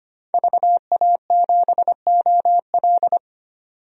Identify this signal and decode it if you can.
Morse code, heard as a keyed tone, about 25 wpm, 700 Hz. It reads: VA7OL
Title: DXCC-100 Call Signs - 5 Characters - 25wpm